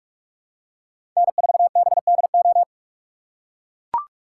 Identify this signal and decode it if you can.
Morse code, heard as a keyed tone, about 43 wpm, 700 Hz. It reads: N4BDO
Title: USA Call Signs - 3x Character Spacing - 45wpm